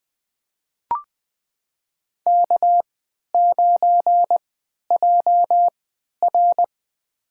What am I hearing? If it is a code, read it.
Morse code, heard as a keyed tone, about 21 wpm, spaced as for 11 wpm, 700 Hz. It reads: K9JR
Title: USA Call Signs - 4 Characters - 3x Character Spacing - 20wpm